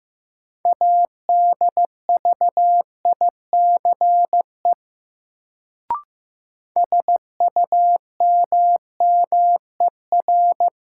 Morse code, heard as a keyed tone, about 15 wpm, 700 Hz. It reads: ADVICE SUMMER
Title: Sentences from Top 1100 Words - New Word Drill Alt - 15wpm